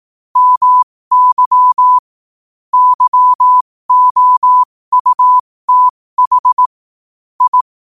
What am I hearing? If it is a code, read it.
Morse code, heard as a keyed tone, about 18 wpm, spaced as for 15 wpm, 1000 Hz. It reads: MY YOUTH I
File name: qotdfiles / 15wpm